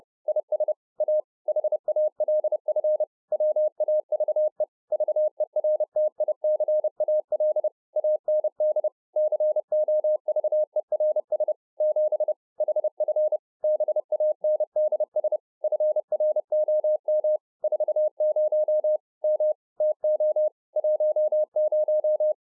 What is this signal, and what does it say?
Morse code, heard as a keyed tone, about 30 wpm, 610 Hz. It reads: IS A HALF WAVE VERTICAL AND COVERS 7 HF BANDS FROM 40 M TO 10